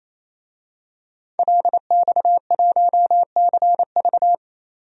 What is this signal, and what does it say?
Morse code, heard as a keyed tone, about 27 wpm, 690 Hz. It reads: LX1CV
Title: DXCC-100 Call Signs - 5 Characters - 28wpm